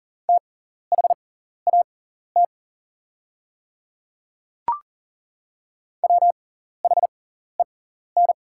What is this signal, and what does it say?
Morse code, heard as a keyed tone, about 40 wpm, spaced as for 14 wpm, 690 Hz. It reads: THAT WHEN
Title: 6x Character Spacing - Top 100 Words - 40wpm